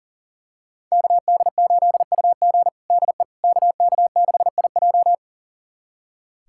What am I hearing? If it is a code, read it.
Morse code, heard as a keyed tone, about 40 wpm, 700 Hz. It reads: KD8UG DE KK6IJ
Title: USA Callsign DE Callsign - 5 Characters - 40wpm